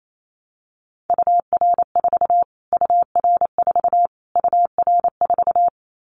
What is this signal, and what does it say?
Morse code, heard as a keyed tone, about 28 wpm, 700 Hz. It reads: UR4 UR4 UR4
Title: Callsign Prefixes for Top 25 DXCC - Repeated 3x - 28wpm